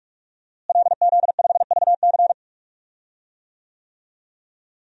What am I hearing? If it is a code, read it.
Morse code, heard as a keyed tone, about 46 wpm, 700 Hz. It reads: LZ5VC
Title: DXCC-100 Call Signs - 5 Characters - Repeated 3x - 45wpm